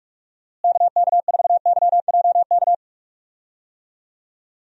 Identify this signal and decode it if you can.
Morse code, heard as a keyed tone, about 44 wpm, 700 Hz. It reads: KK4YJK
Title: USA Call Signs - 6 Characters - Repeated 2x - 45wpm